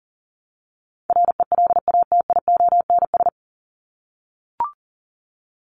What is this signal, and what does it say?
Morse code, heard as a keyed tone, about 40 wpm, 700 Hz. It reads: RELATIONS
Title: ES:S Suffix - Top 100 Words with Base Words 40wpm